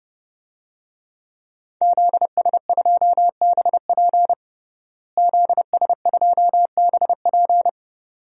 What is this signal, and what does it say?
Morse code, heard as a keyed tone, about 32 wpm, 700 Hz. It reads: ZS2BP ZS2BP
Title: DXCC-100 Call Signs - 5 Characters - Repeated 2x - 30wpm